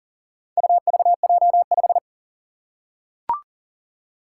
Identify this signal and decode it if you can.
Morse code, heard as a keyed tone, about 40 wpm, 700 Hz. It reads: UVJ5
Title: Four Challenging Letter-Number Combinations 40wpm